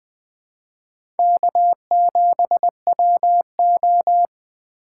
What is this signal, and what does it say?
Morse code, heard as a keyed tone, about 20 wpm, 700 Hz. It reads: K7WO
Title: USA Call Signs - 4 Characters - 4x Character Spacing - 20wpm